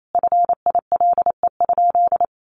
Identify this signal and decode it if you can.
Morse code, heard as a keyed tone, about 28 wpm, 700 Hz. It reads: FILE?
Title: Sentences from Top 1200 Words 28wpm